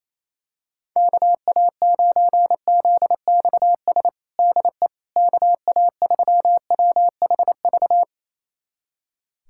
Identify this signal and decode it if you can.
Morse code, heard as a keyed tone, about 27 wpm, 700 Hz. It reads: KA9ZXS DE KA3WHV
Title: USA Callsign DE Callsign - 6 Characters - 28wpm